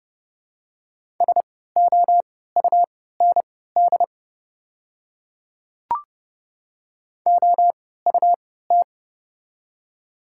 Morse code, heard as a keyed tone, about 32 wpm, spaced as for 17 wpm, 700 Hz. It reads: SOUND OUT
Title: 3x Character Spacing - Top 100 Words - 30wpm